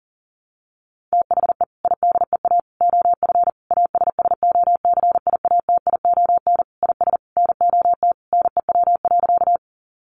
Morse code, heard as a keyed tone, about 40 wpm, 700 Hz. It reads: THE IDEA OF ASSOCIATION IS NOT NEW.
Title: Sentences from Top 1100 Words 40wpm